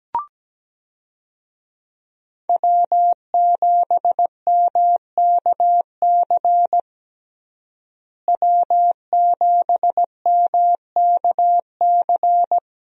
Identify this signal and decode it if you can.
Morse code, heard as a keyed tone, about 17 wpm, 700 Hz. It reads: W7MKC W7MKC
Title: USA Call Signs - 5 Characters - Repeated 3x - 17wpm